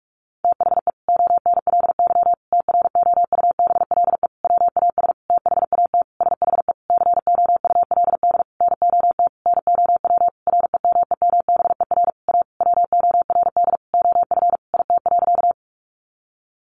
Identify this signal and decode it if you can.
Morse code, heard as a keyed tone, about 44 wpm, 700 Hz. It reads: THE ONLY TROUBLE WAS THAT SHE COULD NOT NOW REMEMBER A WORD OF IT.
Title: Sentences from Top 1200 Words 45wpm